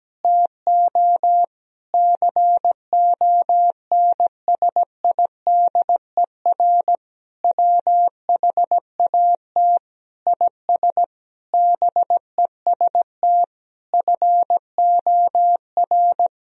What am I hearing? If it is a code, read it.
Morse code, heard as a keyed tone, about 17 wpm, 700 Hz. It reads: TO CONSIDER WHAT IS BEST FOR
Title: Sentences from Top 1000 Words - Review Part 4 - 17wpm